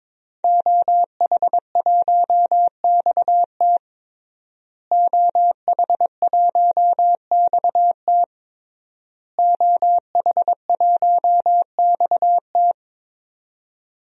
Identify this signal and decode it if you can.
Morse code, heard as a keyed tone, about 21 wpm, 700 Hz. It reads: OH1XT OH1XT OH1XT
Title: DXCC-100 Call Signs - 5 Characters - Repeated 3x - 22wpm